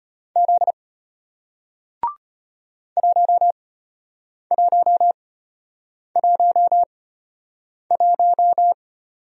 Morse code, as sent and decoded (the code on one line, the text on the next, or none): --.. / .---- .---- .---- .----
Z 1111